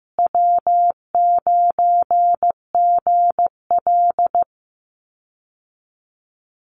Morse code, heard as a keyed tone, about 15 wpm, 700 Hz. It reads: W9GL
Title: Call Signs Repeated 2x - 15wpm